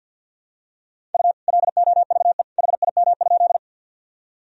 Morse code, heard as a keyed tone, about 52 wpm, 700 Hz. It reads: U LOVE HIM?
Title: Sentences from Top 400 Words and Abbreviations 50wpm